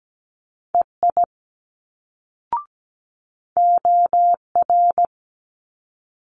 Morse code, heard as a keyed tone, about 17 wpm, 700 Hz. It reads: EI OR
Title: Multi-letter Phonemes Quick Fire 17wpm